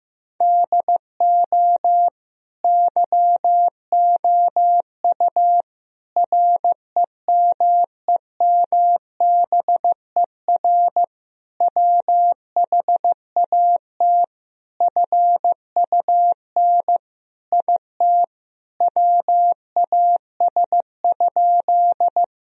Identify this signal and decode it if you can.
Morse code, heard as a keyed tone, about 15 wpm, 700 Hz. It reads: DO YOU REMEMBER WHAT FUN IT WAS?
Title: Sentences from Top 1000 Words - Review Part 4 - 15wpm